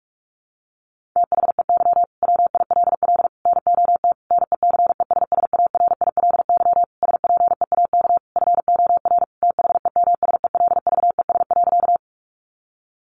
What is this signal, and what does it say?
Morse code, heard as a keyed tone, about 44 wpm, 700 Hz. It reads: THEY WILL NOT NECESSARILY SPEAK FOR THEMSELVES.
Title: Sentences from Top 1400 Words 45wpm